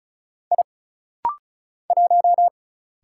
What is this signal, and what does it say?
Morse code, heard as a keyed tone, about 36 wpm, 700 Hz. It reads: I 1
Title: Single Letter-Number - Rapid-Fire 35wpm